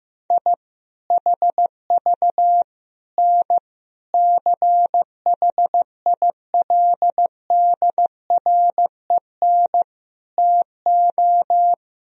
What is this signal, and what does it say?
Morse code, heard as a keyed tone, about 15 wpm, 700 Hz. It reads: I HV N CHILDREN TO
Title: Sentences from Top 400 Words and Abbreviations 15wpm